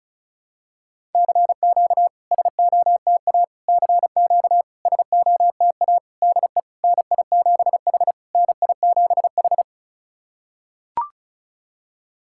Morse code, heard as a keyed tone, about 36 wpm, 700 Hz. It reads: CQ SOTA CQ SOTA DE NI7H NI7H
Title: SOTA QSO Practice 35wpm